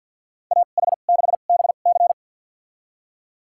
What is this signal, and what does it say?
Morse code, heard as a keyed tone, about 51 wpm, spaced as for 36 wpm, 700 Hz. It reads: AH6BC
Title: USA Call Signs - 5 Characters - 2x Character Spacing - 50wpm